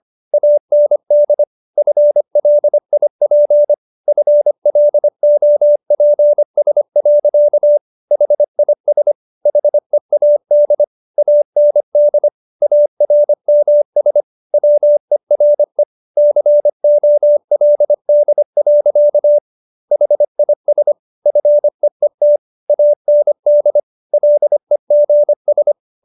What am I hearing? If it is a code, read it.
Morse code, heard as a keyed tone, about 25 wpm, 590 Hz. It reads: AND FLIP FLOPS. HIS HEAD AND ARMS WERE COLD. HIS FEET AND LEGS